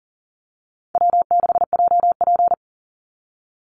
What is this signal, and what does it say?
Morse code, heard as a keyed tone, about 38 wpm, 700 Hz. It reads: W6JP
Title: NA SOTA Chaser Callsigns 40wpm